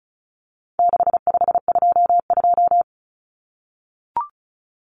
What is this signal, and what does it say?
Morse code, heard as a keyed tone, about 34 wpm, 700 Hz. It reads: 6522
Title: Common Cut Numbers - 4-Digit 35wpm